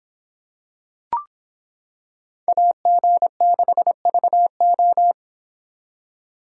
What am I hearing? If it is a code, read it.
Morse code, heard as a keyed tone, about 26 wpm, 700 Hz. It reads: AG6VO